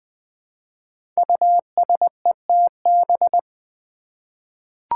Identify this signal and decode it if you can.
Morse code, heard as a keyed tone, about 20 wpm, 700 Hz. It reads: USETB
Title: Five Challenging Letter-Number Combinations 20wpm